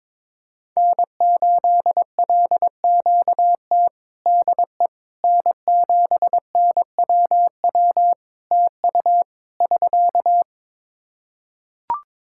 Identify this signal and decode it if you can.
Morse code, heard as a keyed tone, about 22 wpm, 700 Hz. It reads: N8LQT DE N7NWW TU <SK>
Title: Practice QSO - Exchange 4 - Ending - 22wpm